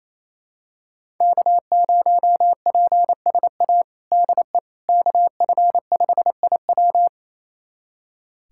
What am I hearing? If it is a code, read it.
Morse code, heard as a keyed tone, about 28 wpm, 700 Hz. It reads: K0PSA DE KF5IW
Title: USA Callsign DE Callsign - 5 Characters - 28wpm